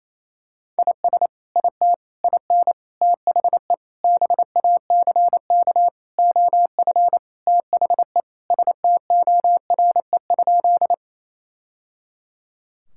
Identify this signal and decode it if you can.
Morse code, heard as a keyed tone, about 27 wpm, 700 Hz. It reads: IS IT IN THE BACK OF THE STORE?
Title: Sentences from Top 1000 Words - Review Part 5 - 28wpm